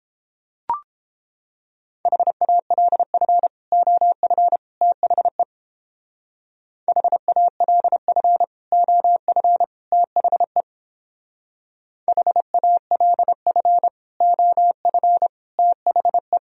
Morse code, heard as a keyed tone, about 29 wpm, 700 Hz. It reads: HALF OF THE HALF OF THE HALF OF THE